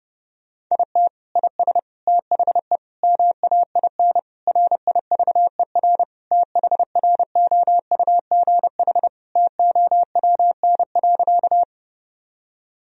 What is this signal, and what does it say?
Morse code, heard as a keyed tone, about 30 wpm, 700 Hz. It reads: IT IS THE MAIN RIVER THROUGH TOWN.
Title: Sentences from Top 1000 Words - Review Part 2 - 30wpm